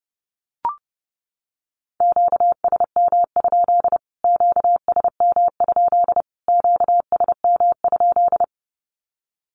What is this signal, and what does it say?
Morse code, heard as a keyed tone, about 30 wpm, 700 Hz. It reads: QSM? QSM? QSM?
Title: Q-Codes Repeated 3x 30wpm